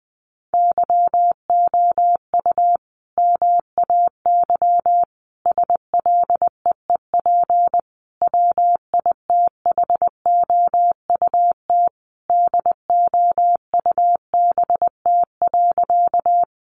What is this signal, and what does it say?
Morse code, heard as a keyed tone, about 20 wpm, 700 Hz. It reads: YOU MAY SLEEP WITHOUT DOUBT.
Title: Sentences from Top 1500 Words 20wpm